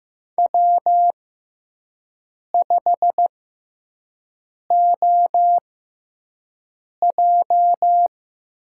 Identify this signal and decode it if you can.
Morse code, heard as a keyed tone, about 16 wpm, spaced as for 5 wpm, 700 Hz. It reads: W5OJ
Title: USA Call Signs - 5 Characters - 6x Character Spacing - 15wpm